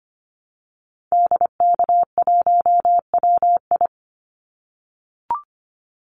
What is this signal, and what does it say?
Morse code, heard as a keyed tone, about 25 wpm, 700 Hz. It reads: DK1WI
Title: Call Signs - 25wpm